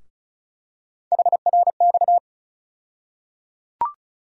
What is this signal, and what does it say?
Morse code, heard as a keyed tone, about 34 wpm, 700 Hz. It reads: HRX
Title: USA Call Sign Postfixes - 3 Letters 35wpm